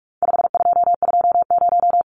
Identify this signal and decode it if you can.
Morse code, heard as a keyed tone, about 44 wpm, 700 Hz. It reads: 5220
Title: Numbers - 4-Digit 45wpm